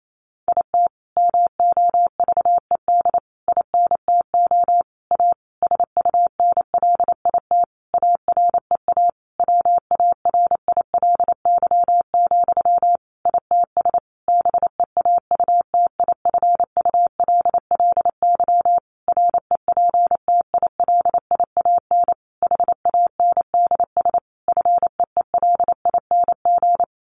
Morse code, heard as a keyed tone, about 28 wpm, 700 Hz. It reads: IT MOVED INTO A SUNLIT AREA WARILY, ITS BEAUTIFULLY REPTILIAN HANDS FEELING
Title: A Sound Of Thunder 28wpm